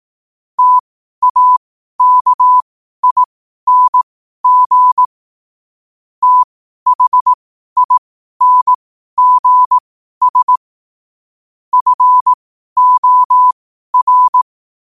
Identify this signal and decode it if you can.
Morse code, heard as a keyed tone, about 18 wpm, spaced as for 12 wpm, 1000 Hz. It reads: TAKING THINGS FOR